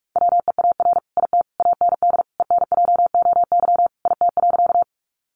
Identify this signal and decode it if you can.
Morse code, heard as a keyed tone, about 44 wpm, 700 Hz. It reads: WEAR IT AND ENJOY IT.
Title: Sentences from Top 1500 Words 45wpm